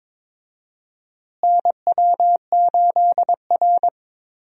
Morse code, heard as a keyed tone, about 21 wpm, 700 Hz. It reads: NW8R
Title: USA Call Signs - 4 Characters - Repeated 2x - 22wpm